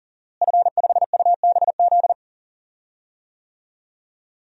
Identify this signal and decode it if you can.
Morse code, heard as a keyed tone, about 39 wpm, 700 Hz. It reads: F5UBZ
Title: DXCC-100 Call Signs - 5 Characters - Repeated 3x - 40wpm